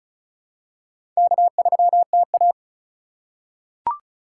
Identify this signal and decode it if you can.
Morse code, heard as a keyed tone, about 36 wpm, 700 Hz. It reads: K3TA
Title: USA Call Signs - 4 Characters - 4x Character Spacing - 35wpm